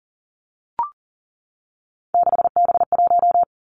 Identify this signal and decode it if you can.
Morse code, heard as a keyed tone, about 39 wpm, 700 Hz. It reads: 6B1
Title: Three Challenging Letter-Number Combinations 40wpm